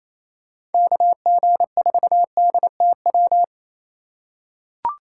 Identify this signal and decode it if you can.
Morse code, heard as a keyed tone, about 28 wpm, 700 Hz. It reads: KG4DTW
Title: USA Call Signs - 3x Character Spacing - 28wpm